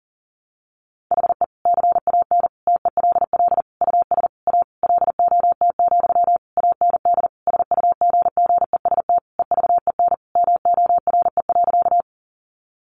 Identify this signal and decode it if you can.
Morse code, heard as a keyed tone, about 40 wpm, 700 Hz. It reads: HE CAN TELL US A LOT, AND SUGGEST EVEN MORE.
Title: Sentences from Top 1300 Words 40wpm